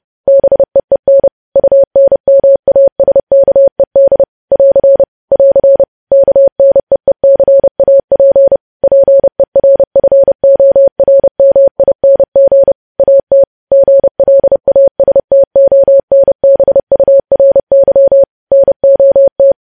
Morse code, heard as a keyed tone, about 30 wpm, 550 Hz. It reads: BEEN UNMASKED <AR> <AR> KNEECAP PERFORMING AT GLASTONBURY NOT
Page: news headlines in morse code at 30 wpm